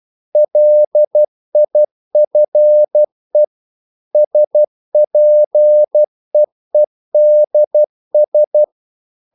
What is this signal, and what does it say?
Morse code, heard as a keyed tone, about 12 wpm, 600 Hz. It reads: LIFE SPEEDS
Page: English quotations in CW (telegraphy - morse)